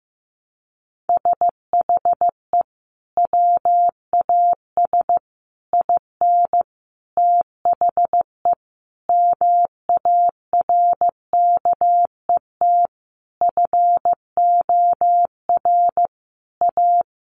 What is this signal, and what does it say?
Morse code, heard as a keyed tone, about 15 wpm, 700 Hz. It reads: SHE WAS IN THE MARKET FOR A
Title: Sentences from Top 600 Words 15wpm